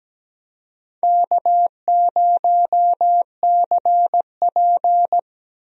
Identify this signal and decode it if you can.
Morse code, heard as a keyed tone, about 17 wpm, 700 Hz. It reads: K0CP
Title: USA Call Signs - 4 Characters - Repeated 3x - 17wpm